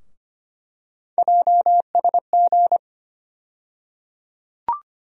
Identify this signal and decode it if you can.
Morse code, heard as a keyed tone, about 25 wpm, 700 Hz. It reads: JSG
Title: USA Call Sign Postfixes - 3 Letters 25wpm